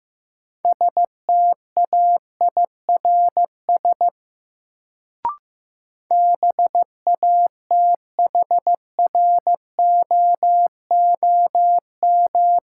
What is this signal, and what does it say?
Morse code, heard as a keyed tone, about 15 wpm, 700 Hz. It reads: STAIRS BATHROOM
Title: Sets of 5 Words Encore - New Word Drill - 15wpm